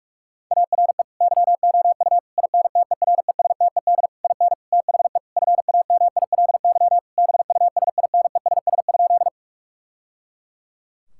Morse code, heard as a keyed tone, about 43 wpm, 700 Hz. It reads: ARE YOU INTERESTED IN THE FAMILY BUSINESS?
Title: Sentences from Top 1000 Words - Review Part 6 - 45wpm